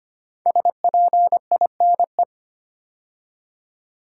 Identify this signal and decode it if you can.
Morse code, heard as a keyed tone, about 25 wpm, 700 Hz. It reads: SPINE
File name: 5 Letter Words - First 3 Letters Given - 25wpm